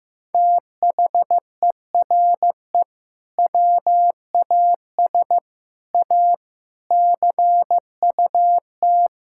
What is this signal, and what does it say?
Morse code, heard as a keyed tone, about 15 wpm, 700 Hz. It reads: THERE WAS A CUT
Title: Sentences from Top 1000 Words - Review Part 5 - 15wpm